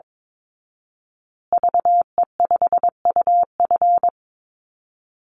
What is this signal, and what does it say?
Morse code, heard as a keyed tone, about 22 wpm, 700 Hz. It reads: VE5UF
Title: Call Signs Repeated 2x - 22wpm